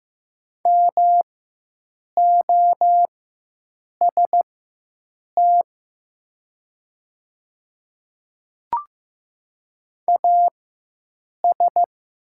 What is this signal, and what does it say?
Morse code, heard as a keyed tone, about 15 wpm, spaced as for 7 wpm, 700 Hz. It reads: MOST AS
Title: 4x Character Spacing - Top 100 Words - 15wpm